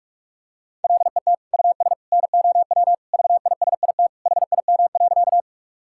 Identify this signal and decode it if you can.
Morse code, heard as a keyed tone, about 41 wpm, 700 Hz. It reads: LET US NOW VISIT HIM.
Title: Sentences from Top 1000 Words - Review Part 6 - 45wpm